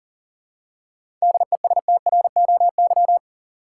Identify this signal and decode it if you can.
Morse code, heard as a keyed tone, about 40 wpm, 700 Hz. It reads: DESTROY
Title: Seven Letter Words - 40wpm